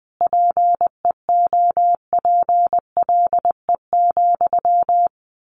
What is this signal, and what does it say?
Morse code, heard as a keyed tone, about 20 wpm, 700 Hz. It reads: PEOPLE,
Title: A Sound Of Thunder 20wpm